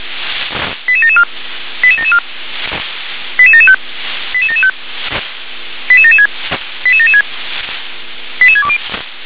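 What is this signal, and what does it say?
Five-tone selective call (ZVEI), heard as quick runs of five tones, tones of about 70 ms. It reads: 88774 88774 88775 88775 88776 88776 88620
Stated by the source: ZVEI five-tone selective call received by SDR.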